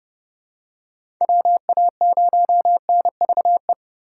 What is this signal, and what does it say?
Morse code, heard as a keyed tone, about 29 wpm, 700 Hz. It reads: WA0NVE
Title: USA Call Signs - 6 Characters - 2x Character Spacing - 30wpm